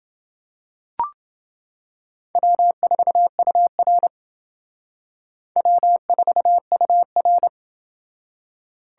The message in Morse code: W4UR W4UR

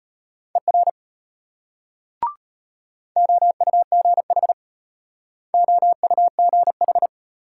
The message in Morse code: ER OUGH OUGH

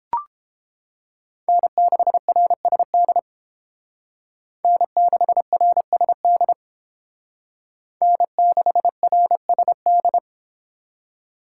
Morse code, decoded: N6RSD N6RSD N6RSD

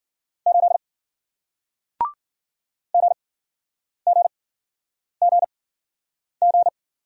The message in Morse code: 8 GGGG

